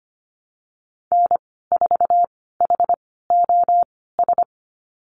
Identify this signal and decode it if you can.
Morse code, heard as a keyed tone, about 25 wpm, spaced as for 16 wpm, 700 Hz. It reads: N4HOS
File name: Lesson 15g - Number 5 - Call Signs 25wpm